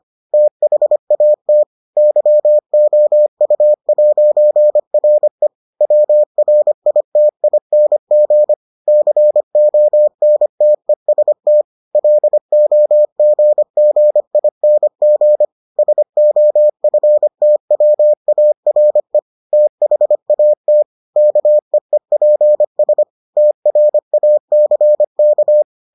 Morse code, heard as a keyed tone, about 25 wpm, 600 Hz. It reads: THAT YOU'RE WRITING CONTEST LOGGING SOFTWARE THAT KEEPS TRACK